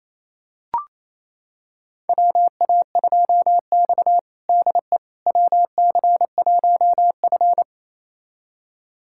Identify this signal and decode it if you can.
Morse code, heard as a keyed tone, about 28 wpm, 700 Hz. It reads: WA2X DE WC1F
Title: USA Callsign DE Callsign - 4 Characters - 28wpm